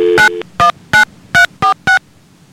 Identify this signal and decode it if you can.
DTMF, heard as telephone touch tones, tones of about 120 ms. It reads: 919346